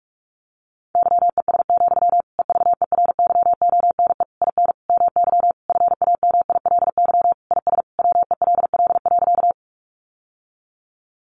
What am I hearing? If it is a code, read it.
Morse code, heard as a keyed tone, about 44 wpm, 700 Hz. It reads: YES, EVERYONE IN MY FAMILY IS WELL.
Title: Sentences from Top 800 Words 45wpm